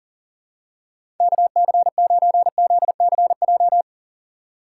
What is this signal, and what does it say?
Morse code, heard as a keyed tone, about 40 wpm, 700 Hz. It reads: KC9ZCJ
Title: USA Call Signs - 6 Characters - 4x Character Spacing - 40wpm